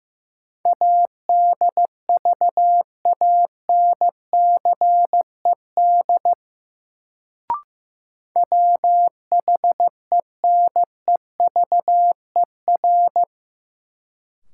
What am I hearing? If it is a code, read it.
Morse code, heard as a keyed tone, about 15 wpm, 700 Hz. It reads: ADVANCED WHENEVER
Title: Sentences from Top 1200 Words - New Word Drill Alt - 15wpm